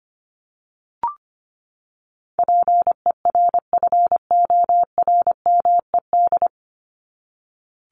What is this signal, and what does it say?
Morse code, heard as a keyed tone, about 25 wpm, 700 Hz. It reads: PERFORMED
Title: ED Suffix - Top 100 Words with Base Words 25wpm